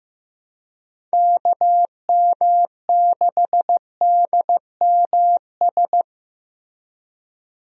Morse code, heard as a keyed tone, about 15 wpm, 700 Hz. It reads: KM6DMS
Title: USA Call Signs - 6 Characters - Repeated 3x - 15wpm